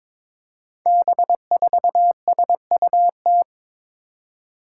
Morse code, heard as a keyed tone, about 22 wpm, 700 Hz. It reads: B4SUT
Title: Five Challenging Letter-Number Combinations 22wpm